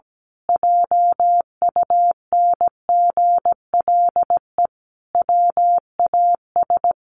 Morse code, heard as a keyed tone, about 17 wpm, 700 Hz. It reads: JUNGLE WAS
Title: A Sound Of Thunder 17wpm